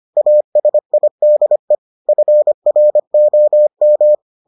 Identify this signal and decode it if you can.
Morse code, heard as a keyed tone, about 25 wpm, 600 Hz. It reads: ASIDE FROM